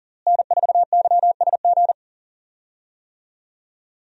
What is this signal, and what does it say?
Morse code, heard as a keyed tone, about 40 wpm, 700 Hz. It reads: N4YSG
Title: USA Call Signs - 5 Characters - Repeated 3x - 40wpm